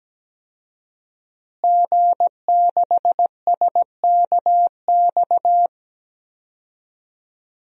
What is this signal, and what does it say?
Morse code, heard as a keyed tone, about 17 wpm, 700 Hz. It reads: G6SKX
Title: DXCC-100 Call Signs - 5 Characters - Repeated 2x - 17wpm